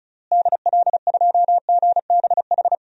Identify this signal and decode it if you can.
Morse code, heard as a keyed tone, about 36 wpm, 700 Hz. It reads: DL2GBH
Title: DXCC-100 Call Signs - 6 Characters - Repeated 3x - 35wpm